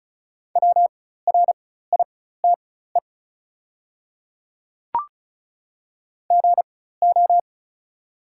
Morse code, heard as a keyed tone, about 34 wpm, spaced as for 16 wpm, 690 Hz. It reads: WRITE GO
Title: 4x Character Spacing - Top 100 Words - 35wpm